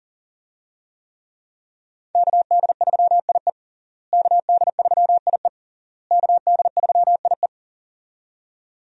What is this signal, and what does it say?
Morse code, heard as a keyed tone, about 40 wpm, 700 Hz. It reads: KD3IE KD3IE KD3IE
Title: USA Call Signs - 5 Characters - Repeated 3x - 40wpm